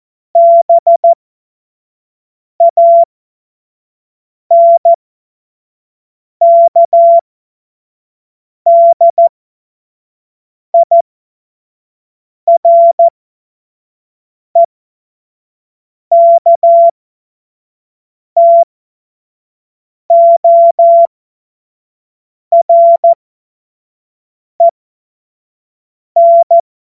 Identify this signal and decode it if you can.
Morse code, heard as a keyed tone, about 14 wpm, spaced as for 5 wpm, 680 Hz. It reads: BANKDIREKTOREN
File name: Morsekurs / Witze / Witz 10 Juni 2025 25 F70